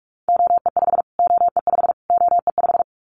Morse code, heard as a keyed tone, about 44 wpm, 700 Hz. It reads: OE5 OE5 OE5
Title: Callsign Prefixes for Top 25 DXCC - Repeated 3x - 45wpm